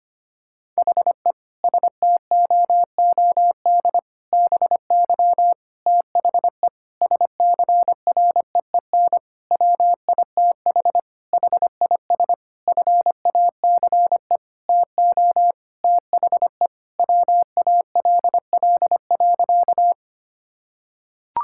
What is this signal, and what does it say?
Morse code, heard as a keyed tone, about 25 wpm, 700 Hz. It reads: HE STOOD BY THE SCREEN WITH HIS FACE TO THE WALL.
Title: Sentences from Top 1000 Words - Review Part 6 - 25wpm